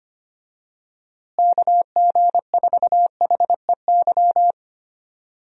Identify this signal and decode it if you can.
Morse code, heard as a keyed tone, about 25 wpm, 700 Hz. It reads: KG4HEY
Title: USA Call Signs - 6 Characters - Repeated 3x - 25wpm